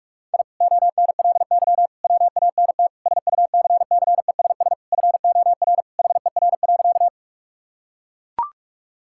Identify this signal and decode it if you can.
Morse code, heard as a keyed tone, about 42 wpm, 700 Hz. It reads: I ONLY WANT SUCCESS FOR HER.
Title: Sentences from Top 1000 Words - Review Part 6 - 45wpm